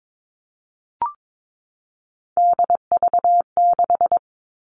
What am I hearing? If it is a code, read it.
Morse code, heard as a keyed tone, about 22 wpm, 700 Hz. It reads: DV6
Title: All Cut Numbers - 3-Digit 22wpm